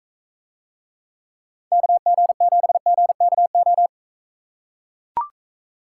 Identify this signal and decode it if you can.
Morse code, heard as a keyed tone, about 42 wpm, 700 Hz. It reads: KG7GKO